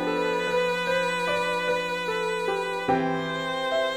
There is no code.